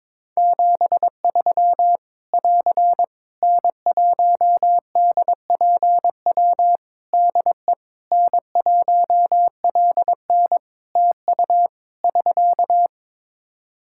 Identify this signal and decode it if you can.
Morse code, heard as a keyed tone, about 21 wpm, 700 Hz. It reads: 73 <AR> N1DPW DE N1LN TU <SK>
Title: Practice QSO - Exchange 4 - Ending - 22wpm